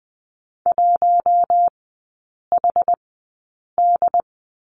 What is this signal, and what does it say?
Morse code, heard as a keyed tone, about 20 wpm, spaced as for 8 wpm, 700 Hz. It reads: 1HD
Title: 067 Lesson 12e - Letter d - Three Characters 20wpm